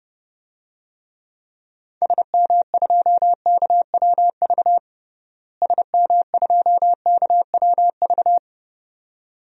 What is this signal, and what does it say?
Morse code, heard as a keyed tone, about 30 wpm, 700 Hz. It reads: SM2KWV SM2KWV
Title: DXCC-100 Call Signs - 6 Characters - Repeated 2x - 30wpm